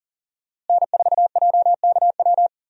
Morse code, heard as a keyed tone, about 40 wpm, 700 Hz. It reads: N4JKW
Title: USA Call Signs - 5 Characters - Repeated 3x - 40wpm